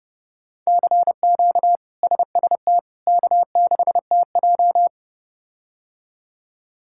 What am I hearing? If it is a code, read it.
Morse code, heard as a keyed tone, about 32 wpm, 700 Hz. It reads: CQ SST K6TJ
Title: K1USN SST QSO Practice - 30wpm